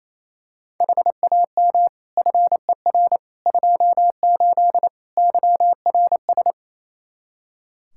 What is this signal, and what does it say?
Morse code, heard as a keyed tone, about 27 wpm, 700 Hz. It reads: HAM FER 28 YRS
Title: Practice QSO - Exchange 3 - More Info - 28wpm